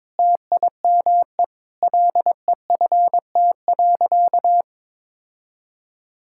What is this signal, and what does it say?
Morse code, heard as a keyed tone, about 22 wpm, 700 Hz. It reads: TIME LEFT.
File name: practice / 22wpm / Sentences with 4-Word Phrases - 22wpm